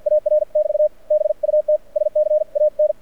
Morse code, heard as a keyed tone, about 56 wpm, spaced as for 46 wpm, 610 Hz. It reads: AR <BT> BUT SCAN
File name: archived / CRSnet